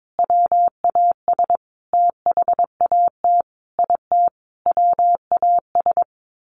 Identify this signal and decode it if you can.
Morse code, heard as a keyed tone, about 22 wpm, 700 Hz. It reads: WAS THAT IT WAS
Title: Sentences from Top 1400 Words 22wpm